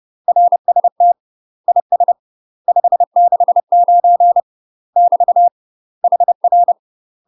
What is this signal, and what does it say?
Morse code, heard as a keyed tone, about 30 wpm, 700 Hz. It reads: RST IS 569 <BT> HR
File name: UK MASTER QSOs TRAINING/30WPM